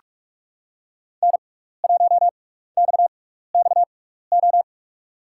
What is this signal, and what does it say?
Morse code, heard as a keyed tone, about 44 wpm, spaced as for 15 wpm, 700 Hz. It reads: N1XXO
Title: USA Call Signs - 5 Characters - 6x Character Spacing - 45wpm